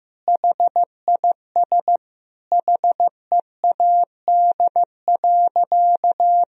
Morse code, heard as a keyed tone, about 15 wpm, 700 Hz. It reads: HIS HEAD.
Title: Sentences from Top 1000 Words - Review Part 2 - 15wpm